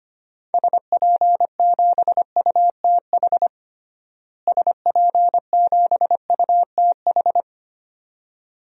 Morse code, heard as a keyed tone, about 25 wpm, 700 Hz. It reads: SP7UTH SP7UTH
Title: DXCC-100 Call Signs - 6 Characters - Repeated 2x - 25wpm